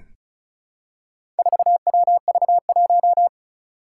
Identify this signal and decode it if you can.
Morse code, heard as a keyed tone, about 36 wpm, 700 Hz. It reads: 4WV1
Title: Four Challenging Letter-Number Combinations 35wpm